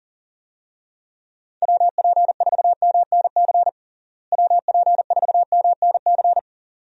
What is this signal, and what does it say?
Morse code, heard as a keyed tone, about 37 wpm, 700 Hz. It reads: WP4MNC WP4MNC
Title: USA Call Signs - 6 Characters - Repeated 2x - 40wpm